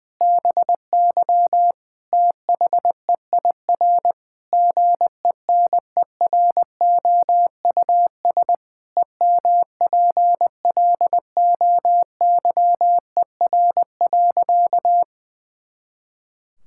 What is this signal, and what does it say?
Morse code, heard as a keyed tone, about 20 wpm, 700 Hz. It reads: BY THEIR GENEROUS EMPLOYER.
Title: Pangram Challenge v2 - 20wpm